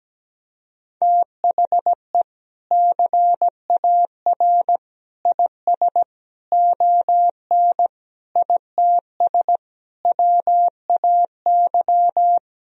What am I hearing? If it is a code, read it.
Morse code, heard as a keyed tone, about 17 wpm, 700 Hz. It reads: THE CAR IS ON ITS WAY
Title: Sentences from Top 1000 Words - Review Part 1 - 17wpm